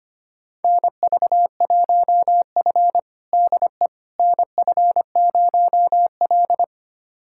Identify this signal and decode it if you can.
Morse code, heard as a keyed tone, about 25 wpm, 700 Hz. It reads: NV1F DE NF0L
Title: USA Callsign DE Callsign - 4 Characters - 25wpm